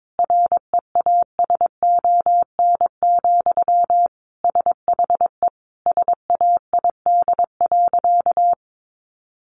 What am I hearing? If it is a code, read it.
Morse code, heard as a keyed tone, about 22 wpm, 700 Hz. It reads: REASON, SHE SAID.
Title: Sentences from Top 700 Words 22wpm